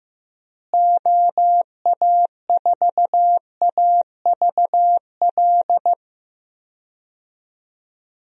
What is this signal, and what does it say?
Morse code, heard as a keyed tone, about 15 wpm, 700 Hz. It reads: OA4AVL
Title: USA Call Signs - 6 Characters - Repeated 3x - 15wpm